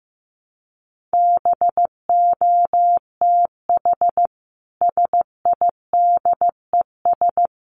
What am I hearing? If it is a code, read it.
Morse code, heard as a keyed tone, about 15 wpm, 700 Hz. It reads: BOTH SIDES
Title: Sets of 5 Words Encore2 15wpm